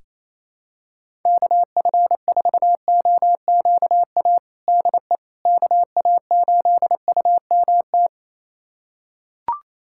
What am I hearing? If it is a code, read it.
Morse code, heard as a keyed tone, about 27 wpm, 700 Hz. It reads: KF4OQA DE KA8UMT
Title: USA Callsign DE Callsign - 6 Characters - 28wpm